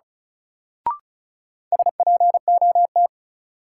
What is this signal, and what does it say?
Morse code, heard as a keyed tone, about 36 wpm, 700 Hz. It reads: SPOT
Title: Sentences from Top 1100 Words - New Word Drill - 35wpm